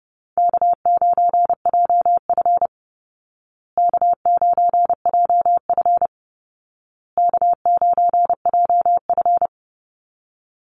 Morse code, heard as a keyed tone, about 30 wpm, 700 Hz. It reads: K9JF K9JF K9JF
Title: Easy Call Signs Repeated 3x - 30wpm